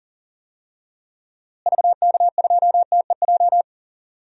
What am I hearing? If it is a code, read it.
Morse code, heard as a keyed tone, about 37 wpm, 700 Hz. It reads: VK2TEJ
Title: DXCC-100 Call Signs - 6 Characters - Repeated 2x - 40wpm